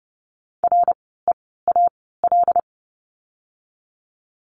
Morse code, heard as a keyed tone, about 30 wpm, spaced as for 17 wpm, 700 Hz. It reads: REAL
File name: Lesson 17f - Letter u - Words 30wpm